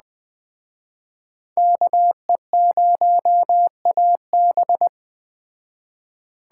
Morse code, heard as a keyed tone, about 20 wpm, 700 Hz. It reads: KE0AB
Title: Speed Racing - USA 2x2 Call Signs 17wpm